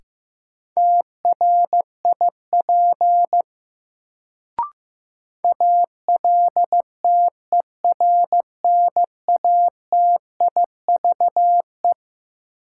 Morse code, heard as a keyed tone, about 15 wpm, 700 Hz. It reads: TRIP ALTERNATIVE
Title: Sentences from Top 1000 Words - Review Part 4 - New Word Drill - 15wpm